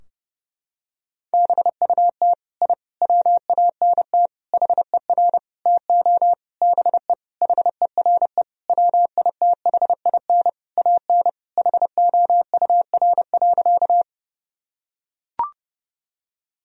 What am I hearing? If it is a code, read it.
Morse code, heard as a keyed tone, about 32 wpm, 700 Hz. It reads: BUT I WANT HER TO BE HERE WITHIN AN HOUR.
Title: Sentences from Top 1000 Words - Review Part 4 - 30wpm